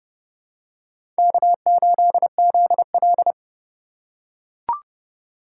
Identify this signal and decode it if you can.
Morse code, heard as a keyed tone, about 30 wpm, 700 Hz. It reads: K8ZL
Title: USA Call Signs - 4 Characters - 2x Character Spacing - 30wpm